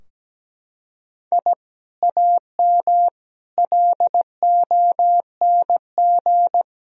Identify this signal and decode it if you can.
Morse code, heard as a keyed tone, about 17 wpm, 700 Hz. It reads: I AM LONG
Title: Sentences from Top 1000 Words - Review Part 3 - 17wpm